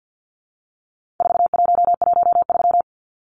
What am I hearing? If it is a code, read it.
Morse code, heard as a keyed tone, about 50 wpm, 700 Hz. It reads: 4113
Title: Numbers - 4-Digit 50wpm